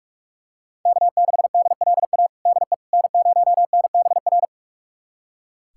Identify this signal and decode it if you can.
Morse code, heard as a keyed tone, about 46 wpm, 700 Hz. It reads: K6DLA DE N0NBR
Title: USA Callsign DE Callsign - 5 Characters - 45wpm